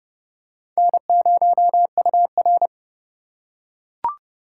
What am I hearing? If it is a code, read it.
Morse code, heard as a keyed tone, about 28 wpm, 700 Hz. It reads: N0UR